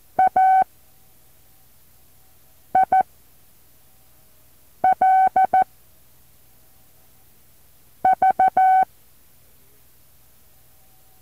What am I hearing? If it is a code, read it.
Morse code, heard as a keyed tone, about 14 wpm, spaced as for 4 wpm, 760 Hz. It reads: AIL V
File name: intl morse code2